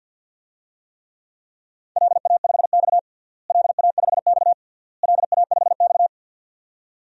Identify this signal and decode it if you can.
Morse code, heard as a keyed tone, about 53 wpm, 700 Hz. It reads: LA5X LA5X LA5X
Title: DXCC-100 Call Signs - 4 Characters - Repeated 3x - 50wpm